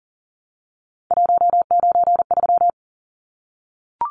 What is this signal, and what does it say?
Morse code, heard as a keyed tone, about 40 wpm, 700 Hz. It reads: 193
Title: Numbers - 3-Digit 40wpm